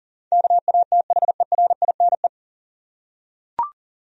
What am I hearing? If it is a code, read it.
Morse code, heard as a keyed tone, about 40 wpm, 700 Hz. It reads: KATHERINE
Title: USA Common Names - 40wpm